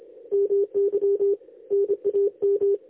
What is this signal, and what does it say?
Morse code, heard as a keyed tone, about 27 wpm, 400 Hz. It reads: MY NAM
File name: lx1nj- -1005utc-10110khz2-49235